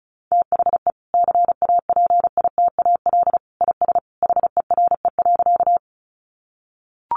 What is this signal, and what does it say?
Morse code, heard as a keyed tone, about 34 wpm, 700 Hz. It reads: THE CAPITAL IS HERE.
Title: Sentences from Top 1000 Words 35wpm